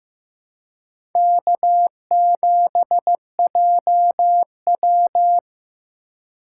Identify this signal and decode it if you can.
Morse code, heard as a keyed tone, about 16 wpm, 700 Hz. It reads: K7JW
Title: USA Call Signs - 4 Characters - 3x Character Spacing - 15wpm